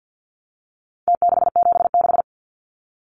Morse code, heard as a keyed tone, about 49 wpm, 700 Hz. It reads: T676
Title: Numbers - Contest Style - 4-Digit 50wpm